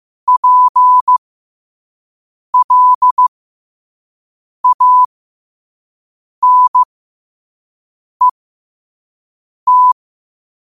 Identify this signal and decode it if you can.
Morse code, heard as a keyed tone, about 15 wpm, spaced as for 5 wpm, 1000 Hz. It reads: PLANET
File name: qotdfiles / 5wpm